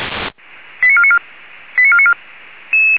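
Five-tone selective call (ZVEI), heard as quick runs of five tones, tones of about 70 ms. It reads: 89494 89494
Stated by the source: ZVEI five-tone selective call received on SDR.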